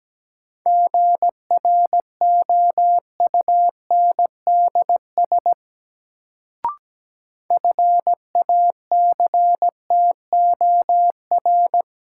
Morse code, heard as a keyed tone, about 17 wpm, 700 Hz. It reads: GROUNDS FACTOR
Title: Sentences from Top 1300 Words - New Word Drill - 17wpm